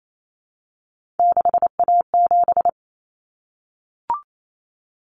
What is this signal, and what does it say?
Morse code, heard as a keyed tone, about 28 wpm, 700 Hz. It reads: 6A7
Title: Common Cut Numbers - 3-Digit 28wpm